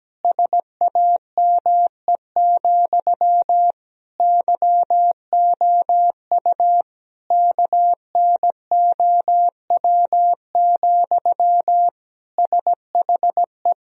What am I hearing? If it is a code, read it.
Morse code, heard as a keyed tone, about 17 wpm, 700 Hz. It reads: SAME, YOU KNOW, SHE
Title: Sentences from Top 1000 Words - Review Part 3 - 17wpm